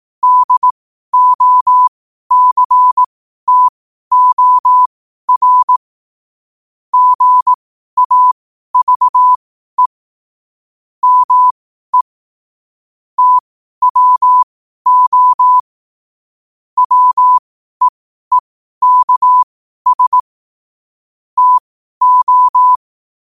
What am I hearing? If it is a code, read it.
Morse code, heard as a keyed tone, about 18 wpm, spaced as for 12 wpm, 1000 Hz. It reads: DOCTOR GAVE ME TWO WEEKS TO